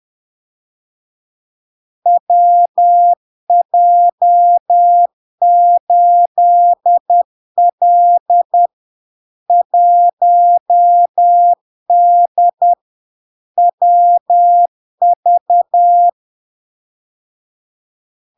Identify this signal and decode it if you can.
Morse code, heard as a keyed tone, about 10 wpm, 700 Hz. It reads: WJ8L 1D WV
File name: LARK-FD-2022-Log-in-CW-10wpm-700-Hz